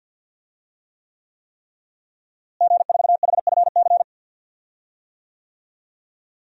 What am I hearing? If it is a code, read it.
Morse code, heard as a keyed tone, about 52 wpm, 700 Hz. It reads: G4HFC
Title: DXCC-100 Call Signs - 5 Characters - Repeated 3x - 50wpm